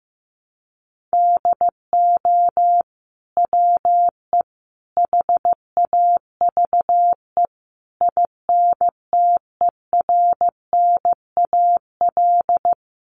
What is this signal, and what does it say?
Morse code, heard as a keyed tone, about 15 wpm, 700 Hz. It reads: DO WE HAVE INTERNAL
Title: Sentences from Top 1400 Words 15wpm